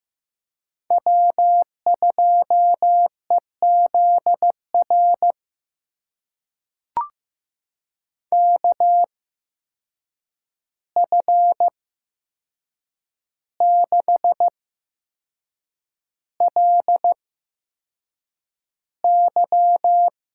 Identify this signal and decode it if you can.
Morse code, heard as a keyed tone, about 15 wpm, spaced as for 6 wpm, 700 Hz. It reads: W2EZR KF6LY
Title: USA Call Signs - 5 Characters - 8x Character Spacing - 15wpm